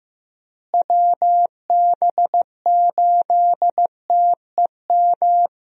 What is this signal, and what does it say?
Morse code, heard as a keyed tone, about 15 wpm, 700 Hz. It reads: WB8TEM
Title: USA Call Signs - 6 Characters - Repeated 3x - 15wpm